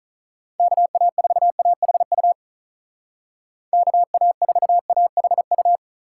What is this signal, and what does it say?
Morse code, heard as a keyed tone, about 43 wpm, 700 Hz. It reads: KA4AHU KA4AHU
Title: Speed Racing - USA 2x3 Call Signs 35wpm